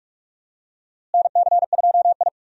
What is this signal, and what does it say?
Morse code, heard as a keyed tone, about 47 wpm, 700 Hz. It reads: NC2I
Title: USA Call Signs - 4 Characters - Repeated 2x - 45wpm